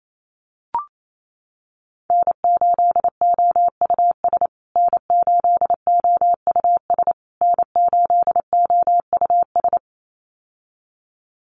Morse code, heard as a keyed tone, about 28 wpm, 700 Hz. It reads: N8OUS N8OUS N8OUS
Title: USA Call Signs Repeated 3x 28wpm